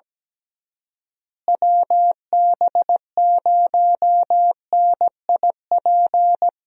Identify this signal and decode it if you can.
Morse code, heard as a keyed tone, about 17 wpm, 700 Hz. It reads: WB0NIP
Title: USA Call Signs - 6 Characters - Repeated 2x - 17wpm